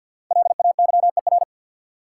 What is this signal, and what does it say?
Morse code, heard as a keyed tone, about 50 wpm, 700 Hz. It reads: LAYER
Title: Sentences from Top 1600 Words - New Word Drill - 50wpm